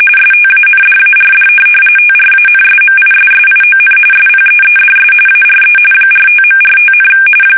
RTTY, about 45.45 baud. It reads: STANDARD TRANSMISSION SPEEDS ARE 45.45,
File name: fsk850 / teletype-RTTY